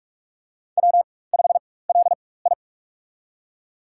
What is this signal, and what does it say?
Morse code, heard as a keyed tone, about 46 wpm, spaced as for 21 wpm, 700 Hz. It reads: W5LI
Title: USA Call Signs - 4 Characters - 4x Character Spacing - 45wpm